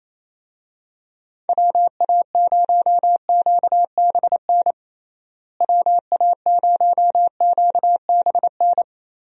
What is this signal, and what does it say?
Morse code, heard as a keyed tone, about 28 wpm, 700 Hz. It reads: WA0QBN WA0QBN
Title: USA Call Signs - 6 Characters - Repeated 3x - 28wpm